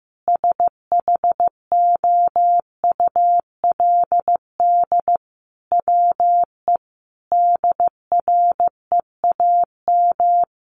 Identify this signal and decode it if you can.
Morse code, heard as a keyed tone, about 15 wpm, 700 Hz. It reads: SHOULD WE DREAM
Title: Sentences from Top 1500 Words 15wpm